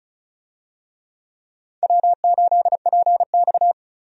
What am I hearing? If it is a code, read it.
Morse code, heard as a keyed tone, about 34 wpm, 700 Hz. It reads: W8PX
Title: USA Call Signs - 4 Characters - 35wpm